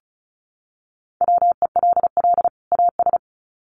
Morse code, heard as a keyed tone, about 34 wpm, 700 Hz. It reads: WELL AS
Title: Sets of 2 Words 35wpm